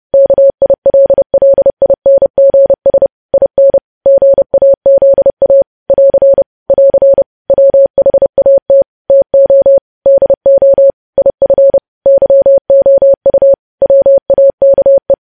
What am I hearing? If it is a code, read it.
Morse code, heard as a keyed tone, about 30 wpm, 550 Hz. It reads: KILLINGS IN GAZA <AR> <AR> WHAT TO DO IF YOU WAKE
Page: news headlines in morse code at 30 wpm